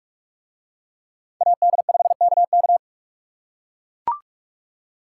Morse code, heard as a keyed tone, about 43 wpm, 700 Hz. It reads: AD5KK